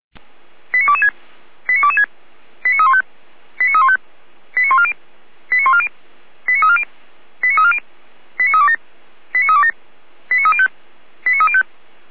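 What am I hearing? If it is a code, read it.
Five-tone selective call (ZVEI), heard as quick runs of five tones, tones of about 70 ms. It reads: 89226 89226 89315 89315 89140 89140 89340 89340 89327 89327 89305 89305
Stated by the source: ZVEI five-tone selective call received via SDR, from the Feuerwehr Orlamünde fire brigade audio server.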